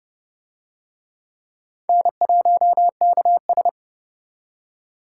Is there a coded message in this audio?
Yes: Morse code from 1.9 to 3.7 s, heard as a keyed tone, about 29 wpm, 700 Hz. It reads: N1KS